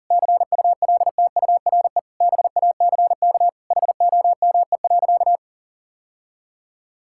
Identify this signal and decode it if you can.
Morse code, heard as a keyed tone, about 40 wpm, 700 Hz. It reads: CULTURE BACK HOME.
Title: Sentences from Top 1000 Words - Review Part 6 - 40wpm